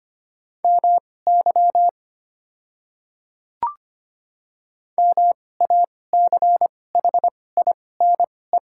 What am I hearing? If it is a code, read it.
Morse code, heard as a keyed tone, about 25 wpm, spaced as for 18 wpm, 700 Hz. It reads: MY MACHINE
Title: 2x Character Spacing - Top 500 Words - 25wpm